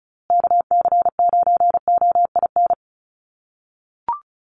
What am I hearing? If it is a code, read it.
Morse code, heard as a keyed tone, about 34 wpm, 700 Hz. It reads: KC9OIN
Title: USA Call Signs Repeated 2x 35wpm